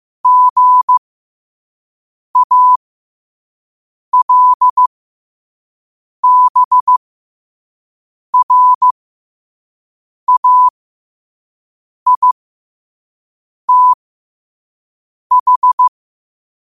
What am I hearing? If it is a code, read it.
Morse code, heard as a keyed tone, about 15 wpm, spaced as for 5 wpm, 1000 Hz. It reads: GALBRAITH
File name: qotdfiles / 5wpm